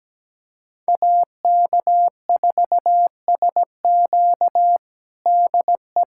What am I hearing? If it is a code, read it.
Morse code, heard as a keyed tone, about 17 wpm, 700 Hz. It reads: AK4SQ DE
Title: USA Callsign DE Callsign - 5 Characters - 17wpm